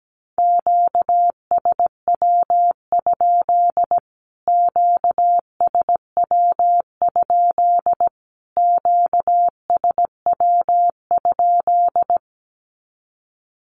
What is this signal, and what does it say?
Morse code, heard as a keyed tone, about 17 wpm, 700 Hz. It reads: QSW? QSW? QSW?
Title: Q-Codes Repeated 3x 17wpm